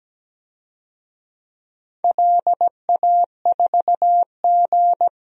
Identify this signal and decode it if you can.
Morse code, heard as a keyed tone, about 17 wpm, 700 Hz. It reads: LA4G
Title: DXCC-100 Call Signs - 4 Characters - Repeated 2x - 17wpm